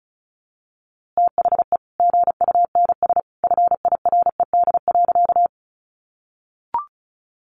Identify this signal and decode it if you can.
Morse code, heard as a keyed tone, about 34 wpm, 700 Hz. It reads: THE GUNS FIRED.
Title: A Sound Of Thunder 35wpm